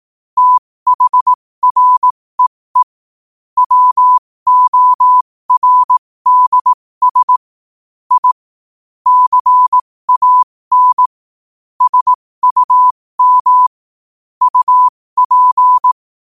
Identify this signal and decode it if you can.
Morse code, heard as a keyed tone, about 18 wpm, spaced as for 15 wpm, 1000 Hz. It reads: THREE WORDS I CAN SUM UP
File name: 15wpm